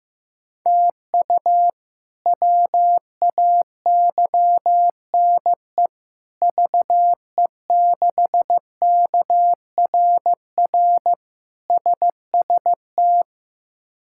Morse code, heard as a keyed tone, about 16 wpm, 700 Hz. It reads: TU WAYNE VE6KRR SST
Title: K1USN SST QSO Practice - 15wpm